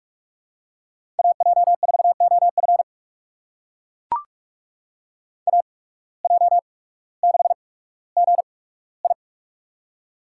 Morse code, heard as a keyed tone, about 43 wpm, spaced as for 19 wpm, 700 Hz. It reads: AJ4OF AJ6GI
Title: USA Call Signs - 5 Characters - 8x Character Spacing - 45wpm